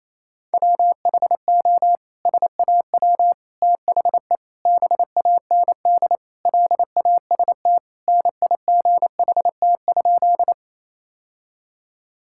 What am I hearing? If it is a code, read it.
Morse code, heard as a keyed tone, about 28 wpm, 700 Hz. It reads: WHO SAW THE BAND LAST NIGHT?
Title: Sentences from Top 1600 Words 28wpm